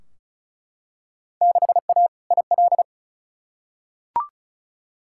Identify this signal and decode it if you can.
Morse code, heard as a keyed tone, about 36 wpm, 700 Hz. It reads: 6A IL
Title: Field Day Exchanges - 35wpm